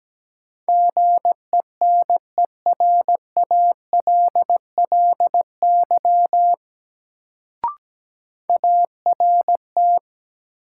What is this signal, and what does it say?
Morse code, heard as a keyed tone, about 17 wpm, 700 Hz. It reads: GENERALLY ART
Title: Sentences from Top 600 Words - New Word Drill - 17wpm